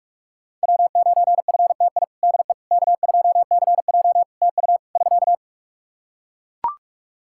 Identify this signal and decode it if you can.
Morse code, heard as a keyed tone, about 41 wpm, 700 Hz. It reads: W9FTI DE K2CJ TU <SK>
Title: K1USN SST QSO Practice 45wpm